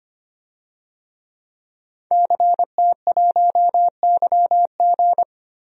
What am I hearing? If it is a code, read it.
Morse code, heard as a keyed tone, about 25 wpm, 700 Hz. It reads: CT1YG
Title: DXCC-100 Call Signs - 5 Characters - 25wpm